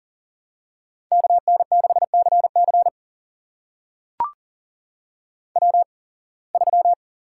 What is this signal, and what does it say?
Morse code, heard as a keyed tone, about 42 wpm, spaced as for 26 wpm, 700 Hz. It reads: KN6CC W3
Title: USA Call Signs - 5 Characters - 8x Character Spacing - 40wpm